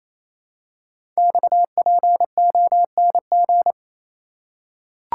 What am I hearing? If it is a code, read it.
Morse code, heard as a keyed tone, about 27 wpm, 700 Hz. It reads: XPONG
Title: Code Groups - 5 Letters - 28wpm